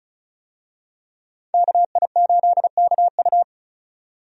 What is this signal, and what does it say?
Morse code, heard as a keyed tone, about 36 wpm, 700 Hz. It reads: KI8KU
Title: USA Call Signs - 5 Characters - 35wpm